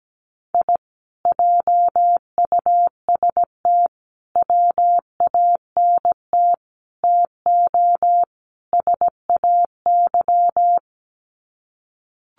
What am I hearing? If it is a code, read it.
Morse code, heard as a keyed tone, about 17 wpm, 700 Hz. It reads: I JUST WANT TO SAY
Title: Sets of 5 Words Encore 17wpm